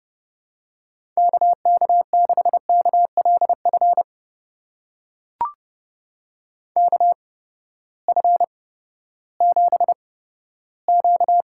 Morse code, heard as a keyed tone, about 30 wpm, spaced as for 15 wpm, 700 Hz. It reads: KK6KLF KF7Q
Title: USA Call Signs - 6 Characters - 8x Character Spacing - 30wpm